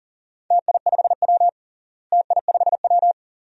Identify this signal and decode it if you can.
Morse code, heard as a keyed tone, about 40 wpm, 700 Hz. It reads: TI5W TI5W
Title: DXCC-100 Call Signs - 4 Characters - Repeated 2x - 40wpm